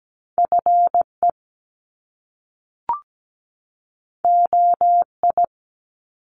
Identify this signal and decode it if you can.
Morse code, heard as a keyed tone, about 17 wpm, 700 Hz. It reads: FE OI
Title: Top Two Letter Combinations in Words 17wpm